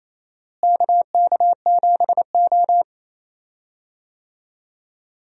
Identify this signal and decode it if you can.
Morse code, heard as a keyed tone, about 28 wpm, 700 Hz. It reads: KK7O
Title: USA Call Signs - 4 Characters - Repeated 3x - 28wpm